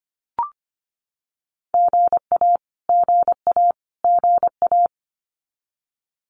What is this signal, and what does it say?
Morse code, heard as a keyed tone, about 25 wpm, 700 Hz. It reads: GA GA GA
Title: US States Repeated 3x 25wpm